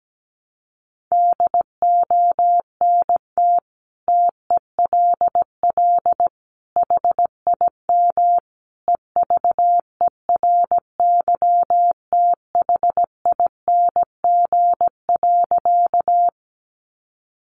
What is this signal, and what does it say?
Morse code, heard as a keyed tone, about 17 wpm, 700 Hz. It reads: DONT TELL HIM EVERYTHING.
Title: Sentences from Top 700 Words 17wpm